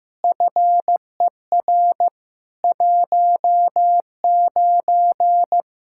Morse code, heard as a keyed tone, about 15 wpm, 700 Hz. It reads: FER 19
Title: Practice QSO - Exchange 3 - More Info - 15wpm